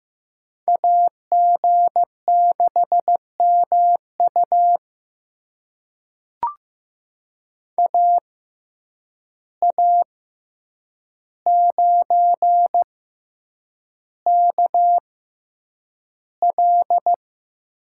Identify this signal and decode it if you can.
Morse code, heard as a keyed tone, about 15 wpm, spaced as for 8 wpm, 700 Hz. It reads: AG6MU AA9KL
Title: USA Call Signs - 5 Characters - 6x Character Spacing - 15wpm